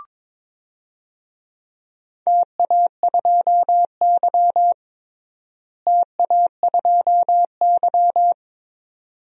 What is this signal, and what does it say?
Morse code, heard as a keyed tone, about 22 wpm, 700 Hz. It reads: TA2Y TA2Y
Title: DXCC-100 Call Signs - 4 Characters - Repeated 2x - 22wpm